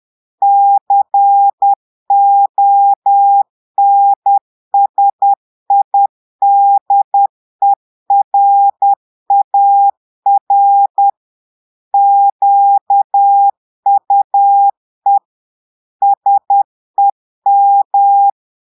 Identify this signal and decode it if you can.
Morse code, heard as a keyed tone, about 10 wpm, 800 Hz. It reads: CONSIDERAR QUE SEM
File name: portugais texte en clair 2 vitesse 10 mots minutes